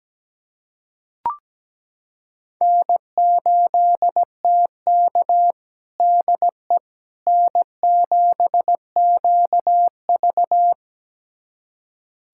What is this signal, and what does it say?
Morse code, heard as a keyed tone, about 17 wpm, 700 Hz. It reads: N8TK DE N7QV
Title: USA Callsign DE Callsign - 4 Characters - 17wpm